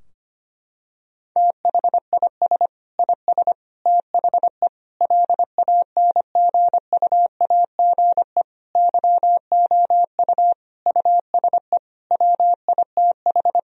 Morse code, heard as a keyed tone, about 25 wpm, 700 Hz. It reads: THIS IS THE LANGUAGE YOU USE WITH
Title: Sentences from Top 1000 Words - Review Part 3 - 25wpm